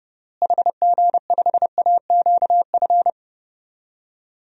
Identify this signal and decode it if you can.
Morse code, heard as a keyed tone, about 32 wpm, 700 Hz. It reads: HG5AQF
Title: DXCC-100 Call Signs - 6 Characters - Repeated 3x - 30wpm